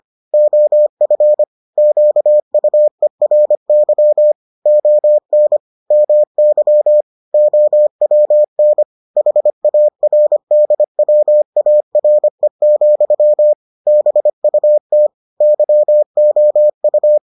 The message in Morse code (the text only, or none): OF QUERY ON MY OWN HARDWARE, BUT YOU